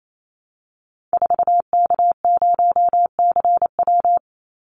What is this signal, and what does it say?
Morse code, heard as a keyed tone, about 28 wpm, 700 Hz. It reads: 4K0CW
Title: Call Signs Repeated 3x - 28wpm